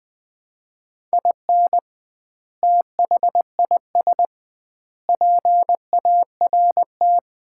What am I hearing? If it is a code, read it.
Morse code, heard as a keyed tone, about 20 wpm, 700 Hz. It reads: IN THIS PART
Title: 2x Word-Spacing - Sets of 5 Words - 20wpm